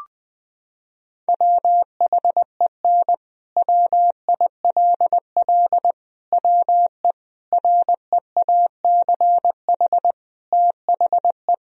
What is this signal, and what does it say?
Morse code, heard as a keyed tone, about 20 wpm, 700 Hz. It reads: WHEN WILL WE REACH THE
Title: Sentences from Top 1000 Words - Review Part 4 - 20wpm